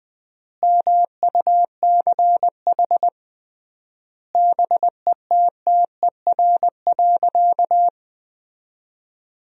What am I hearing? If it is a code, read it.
Morse code, heard as a keyed tone, about 20 wpm, 700 Hz. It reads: MUCH BETTER.
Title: 3x Word-Spacing - Sentences from Top 400 Words - 20wpm